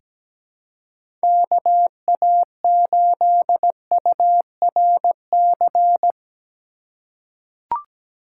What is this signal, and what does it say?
Morse code, heard as a keyed tone, about 17 wpm, 700 Hz. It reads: KA8URC
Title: USA Call Signs - 6 Characters - 4x Character Spacing - 17wpm